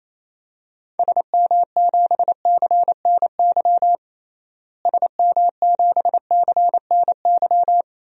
Morse code, heard as a keyed tone, about 28 wpm, 700 Hz. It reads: SM7CNY SM7CNY
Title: DXCC-100 Call Signs - 6 Characters - Repeated 3x - 28wpm